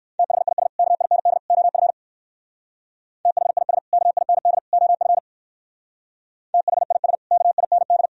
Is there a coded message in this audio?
Yes: Morse code from 0.2 to 8.1 s, heard as a keyed tone, about 56 wpm, 700 Hz. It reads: THIS KIND OF THIS KIND OF THIS KIND